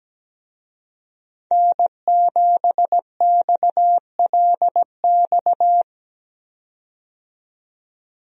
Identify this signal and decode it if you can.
Morse code, heard as a keyed tone, about 17 wpm, 700 Hz. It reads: N7XLX
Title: USA Call Signs - 5 Characters - 17wpm